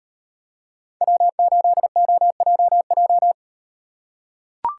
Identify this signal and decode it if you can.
Morse code, heard as a keyed tone, about 38 wpm, 700 Hz. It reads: W8OJJ